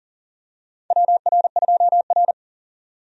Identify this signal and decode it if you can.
Morse code, heard as a keyed tone, about 38 wpm, 700 Hz. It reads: WR2R
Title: USA Call Signs - 4 Characters - 40wpm